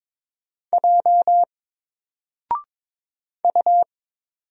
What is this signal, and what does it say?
Morse code, heard as a keyed tone, about 22 wpm, 700 Hz. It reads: J U
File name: Single-Letter-Number-22wpm